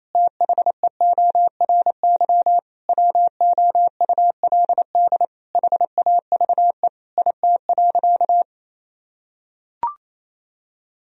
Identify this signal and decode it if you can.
Morse code, heard as a keyed tone, about 27 wpm, 700 Hz. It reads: THEORY WOULD HAVE IT.
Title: Sentences from Top 1000 Words - Review Part 3 - 28wpm